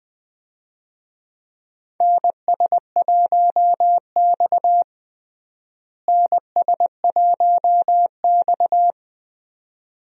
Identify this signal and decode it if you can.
Morse code, heard as a keyed tone, about 21 wpm, 700 Hz. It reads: NS1X NS1X
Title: USA Call Signs - 4 Characters - Repeated 2x - 20wpm